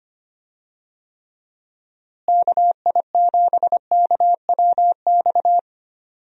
Morse code, heard as a keyed tone, about 25 wpm, 700 Hz. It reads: KI7KWX
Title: USA Call Signs - 6 Characters - Repeated 3x - 25wpm